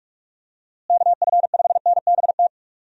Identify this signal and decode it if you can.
Morse code, heard as a keyed tone, about 46 wpm, 700 Hz. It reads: KF5NBT
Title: USA Call Signs - 6 Characters - Repeated 3x - 45wpm